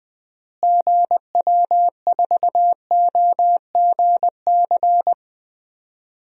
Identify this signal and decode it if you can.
Morse code, heard as a keyed tone, about 20 wpm, 700 Hz. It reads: GW4OGC
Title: DXCC-100 Call Signs - 6 Characters - Repeated 3x - 20wpm